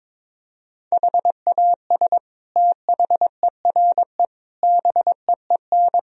Morse code, heard as a keyed tone, about 22 wpm, 700 Hz. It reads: HAS THERE BEEN
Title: Sentences from Top 1000 Words - Review Part 4 - 22wpm